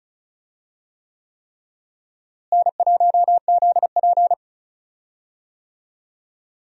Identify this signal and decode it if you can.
Morse code, heard as a keyed tone, about 34 wpm, 700 Hz. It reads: N1ZP
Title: USA Call Signs - 4 Characters - Repeated 3x - 35wpm